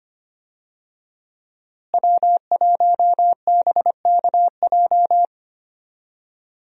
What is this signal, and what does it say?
Morse code, heard as a keyed tone, about 25 wpm, 700 Hz. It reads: W1BKJ
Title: USA Call Signs - 5 Characters - Repeated 2x - 25wpm